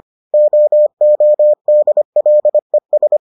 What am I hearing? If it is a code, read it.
Morse code, heard as a keyed tone, about 25 wpm, 600 Hz. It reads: OODLES